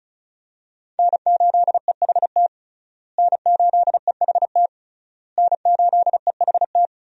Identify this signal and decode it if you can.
Morse code, heard as a keyed tone, about 36 wpm, 700 Hz. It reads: N8EHT N8EHT N8EHT
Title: USA Call Signs - 5 Characters - Repeated 3x - 35wpm